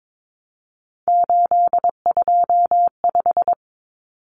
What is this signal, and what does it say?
Morse code, heard as a keyed tone, about 22 wpm, 700 Hz. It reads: 825
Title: Common Cut Numbers - 3-Digit 22wpm